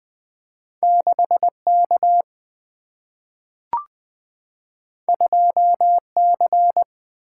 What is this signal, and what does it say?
Morse code, heard as a keyed tone, about 20 wpm, 700 Hz. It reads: 6K 2C
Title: Single-Letter Single-Number 20wpm